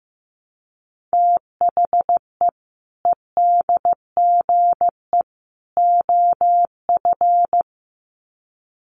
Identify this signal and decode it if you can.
Morse code, heard as a keyed tone, about 15 wpm, 700 Hz. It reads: THE EDGE OF
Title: Sets of 3 Words 15wpm